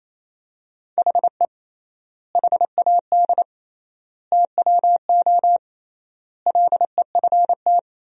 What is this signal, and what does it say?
Morse code, heard as a keyed tone, about 28 wpm, 700 Hz. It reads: HE HAD TWO LEFT
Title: 3x Word-Spacing - Sentences from Top 300 Words - 28wpm